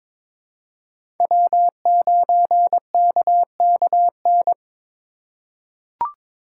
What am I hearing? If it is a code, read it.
Morse code, heard as a keyed tone, about 22 wpm, 700 Hz. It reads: W9KKN
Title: USA Call Signs - 5 Characters - 3x Character Spacing - 22wpm